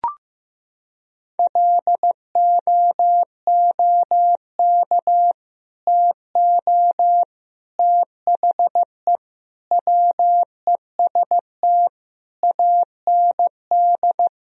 Morse code, heard as a keyed tone, about 16 wpm, 700 Hz. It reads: LOOK TO THE WEST AND
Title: Sentences from Top 1000 Words - Review Part 2 - 15wpm